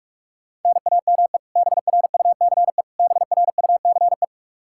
Morse code, heard as a keyed tone, about 41 wpm, 700 Hz. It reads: NAME BRUCE BRUCE
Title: Practice QSO - Exchange 1 - Beginning - 45wpm